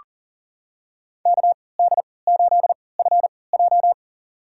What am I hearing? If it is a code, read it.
Morse code, heard as a keyed tone, about 37 wpm, spaced as for 22 wpm, 690 Hz. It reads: KD8FJ
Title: USA Call Signs - 5 Characters - 3x Character Spacing - 40wpm